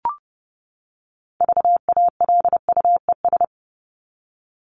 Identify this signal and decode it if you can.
Morse code, heard as a keyed tone, about 30 wpm, 700 Hz. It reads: VALUES
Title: ES:S Suffix - Top 100 Words with Base Words 30wpm